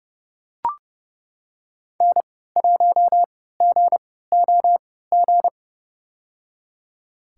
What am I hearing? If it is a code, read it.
Morse code, heard as a keyed tone, about 30 wpm, spaced as for 17 wpm, 700 Hz. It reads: N1GOG
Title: USA Call Signs - 3x Character Spacing - 30wpm